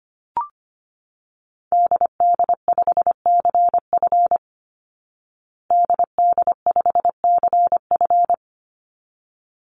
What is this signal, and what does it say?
Morse code, heard as a keyed tone, about 25 wpm, 700 Hz. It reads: DD5CF DD5CF
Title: Call Signs Repeated 3x - 25wpm